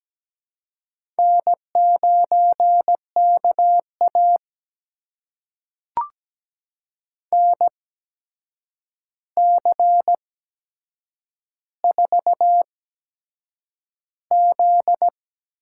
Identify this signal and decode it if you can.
Morse code, heard as a keyed tone, about 17 wpm, spaced as for 7 wpm, 700 Hz. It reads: N9KA NC4Z
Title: USA Call Signs - 4 Characters - 8x Character Spacing - 17wpm